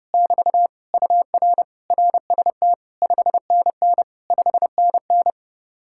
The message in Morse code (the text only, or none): <BT> UR RST 5NN 5NN